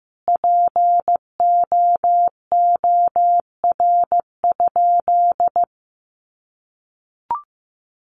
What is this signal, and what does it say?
Morse code, heard as a keyed tone, about 15 wpm, 700 Hz. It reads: POOR?
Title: Sentences from Top 1100 Words 15wpm